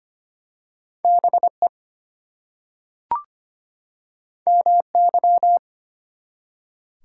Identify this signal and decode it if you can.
Morse code, heard as a keyed tone, about 25 wpm, 700 Hz. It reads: BE MY
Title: Two Letter Words - 25wpm